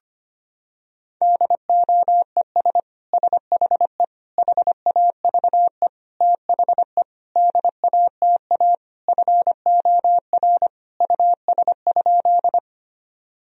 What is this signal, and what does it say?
Morse code, heard as a keyed tone, about 25 wpm, 700 Hz. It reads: DOES SHE HAVE THE DATA FOR US?
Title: Sentences from Top 1000 Words - Review Part 3 - 25wpm